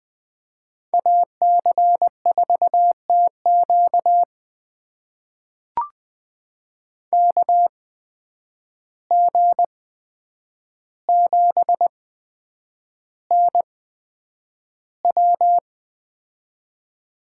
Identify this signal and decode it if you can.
Morse code, heard as a keyed tone, about 20 wpm, spaced as for 9 wpm, 700 Hz. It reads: AC4TQ KG7NW
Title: USA Call Signs - 5 Characters - 8x Character Spacing - 20wpm